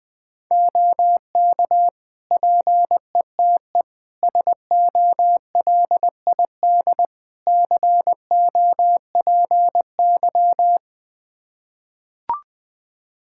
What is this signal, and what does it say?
Morse code, heard as a keyed tone, about 20 wpm, 700 Hz. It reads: OK PETE SOLID COPY
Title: Practice QSO - Exchange 3 - More Info - 20wpm